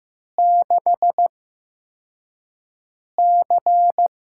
Morse code, heard as a keyed tone, about 15 wpm, spaced as for 4 wpm, 700 Hz. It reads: 6C
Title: USA Call Signs - 4 Characters - 8x Character Spacing - 15wpm